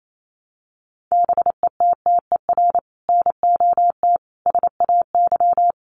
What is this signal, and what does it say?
Morse code, heard as a keyed tone, about 28 wpm, 700 Hz. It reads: BETTER NOT SAY
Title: Sentences from Top 1000 Words 28wpm